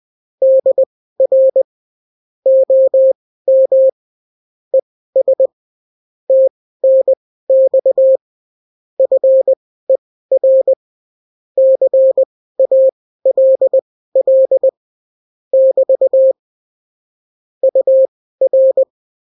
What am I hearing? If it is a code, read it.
Morse code, heard as a keyed tone, about 20 wpm, spaced as for 14 wpm, 530 Hz. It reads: DR OM ES TNX FER CALL <BT> UR
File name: CW QSO libroCW4 20-10ppm